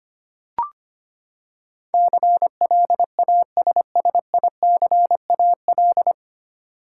Morse code, heard as a keyed tone, about 25 wpm, 700 Hz. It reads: CLASSICAL
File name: Nine Letter Words - 25wpm